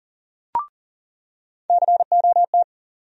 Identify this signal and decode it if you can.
Morse code, heard as a keyed tone, about 40 wpm, 700 Hz. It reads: COT
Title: USA Call Sign Postfixes - 3 Letters 40wpm